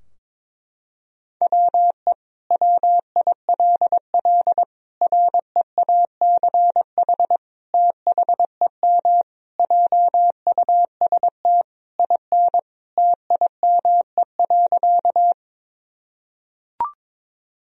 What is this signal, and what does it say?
Morse code, heard as a keyed tone, about 21 wpm, 700 Hz. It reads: WE WILL REACH THEM JUST IN TIME.
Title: Sentences from Top 1000 Words - Review Part 2 - 22wpm